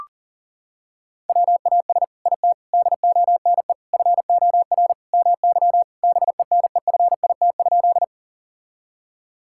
Morse code, heard as a keyed tone, about 40 wpm, 700 Hz. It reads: WAS IT DONE FOR MY BENEFIT?
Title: Sentences from Top 1000 Words - Review Part 6 - 40wpm